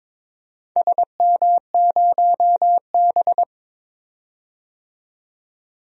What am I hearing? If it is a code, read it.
Morse code, heard as a keyed tone, about 22 wpm, 700 Hz. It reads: SM0B
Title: DXCC-100 Call Signs - 4 Characters - 22wpm